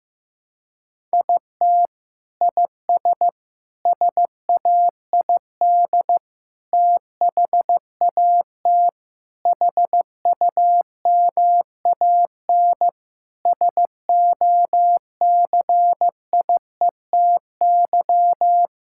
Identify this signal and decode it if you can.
Morse code, heard as a keyed tone, about 15 wpm, 700 Hz. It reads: IT IS SAID THAT HUMAN SOCIETY
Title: Sentences from Top 1000 Words - Review Part 6 - 15wpm